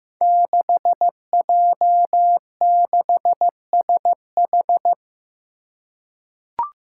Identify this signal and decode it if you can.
Morse code, heard as a keyed tone, about 15 wpm, 700 Hz. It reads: 6J6SH
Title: Five Challenging Letter-Number Combinations 15wpm